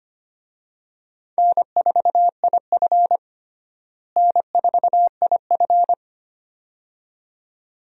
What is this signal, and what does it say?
Morse code, heard as a keyed tone, about 25 wpm, 700 Hz. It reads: N4IF N4IF
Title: USA Call Signs - 4 Characters - Repeated 2x - 25wpm